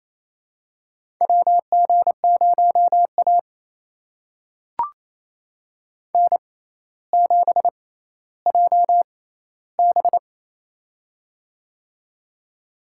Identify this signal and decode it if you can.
Morse code, heard as a keyed tone, about 28 wpm, spaced as for 14 wpm, 700 Hz. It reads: WG0A N7JB
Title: USA Call Signs - 4 Characters - 6x Character Spacing - 28wpm